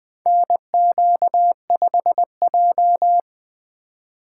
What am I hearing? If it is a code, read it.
Morse code, heard as a keyed tone, about 20 wpm, 700 Hz. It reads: NQ5J
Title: USA Call Signs - 4x Character Spacing - 20wpm